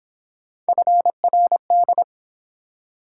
Morse code, heard as a keyed tone, about 28 wpm, 700 Hz. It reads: FRD